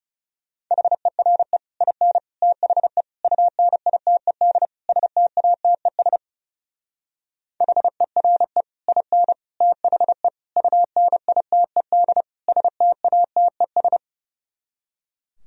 Morse code, heard as a keyed tone, about 35 wpm, 700 Hz. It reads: HERE IN THE UNITED STATES HERE IN THE UNITED STATES
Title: Speed-Racing - Sets of 5 Words 30wpm